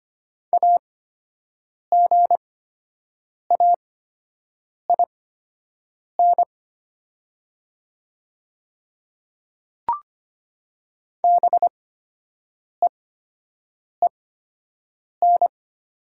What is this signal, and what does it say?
Morse code, heard as a keyed tone, about 25 wpm, spaced as for 7 wpm, 700 Hz. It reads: AGAIN BEEN
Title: 8x Character Spacing - Top 500 Words - 25wpm